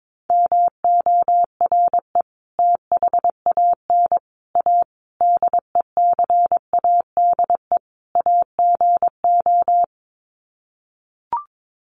Morse code, heard as a keyed tone, about 22 wpm, 700 Hz. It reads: MORE THAN A DECADE AGO
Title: Sets of 5 Words Encore2 22wpm